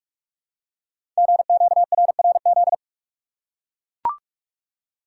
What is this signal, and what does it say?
Morse code, heard as a keyed tone, about 44 wpm, 700 Hz. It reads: GQRRZ
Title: Code Groups - 5 Letters - 45wpm